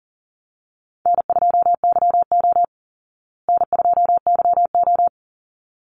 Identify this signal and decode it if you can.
Morse code, heard as a keyed tone, about 40 wpm, 700 Hz. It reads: N2YO N2YO
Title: Easy Call Signs Repeated 2x - 40wpm